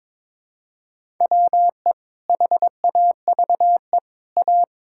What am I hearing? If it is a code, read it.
Morse code, heard as a keyed tone, about 22 wpm, 700 Hz. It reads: WE HAVE A
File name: practice / 22wpm / Sentences with 2-Word Phrases - 22wpm